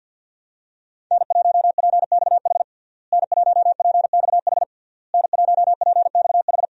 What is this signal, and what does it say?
Morse code, heard as a keyed tone, about 47 wpm, 700 Hz. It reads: N1PXH N1PXH N1PXH
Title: USA Call Signs - 5 Characters - Repeated 3x - 50wpm